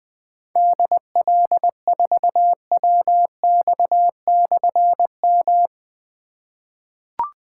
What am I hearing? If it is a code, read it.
Morse code, heard as a keyed tone, about 20 wpm, 700 Hz. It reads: DL4WX/M
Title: Difficult Call Signs 20wpm